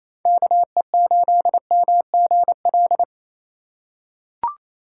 Morse code, heard as a keyed tone, about 28 wpm, 700 Hz. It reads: KE8MGL
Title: USA Call Signs - 3x Character Spacing - 28wpm